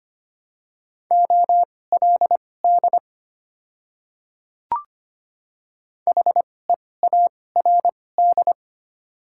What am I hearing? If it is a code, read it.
Morse code, heard as a keyed tone, about 25 wpm, spaced as for 18 wpm, 700 Hz. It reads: OLD HEARD
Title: 2x Character Spacing - Top 500 Words - 25wpm